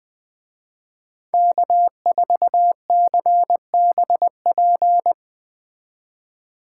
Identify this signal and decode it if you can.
Morse code, heard as a keyed tone, about 20 wpm, 700 Hz. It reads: K4CBP